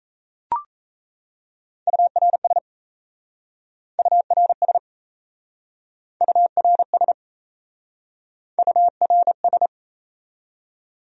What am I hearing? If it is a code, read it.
Morse code, heard as a keyed tone, about 37 wpm, 700 Hz. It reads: URS URS URS URS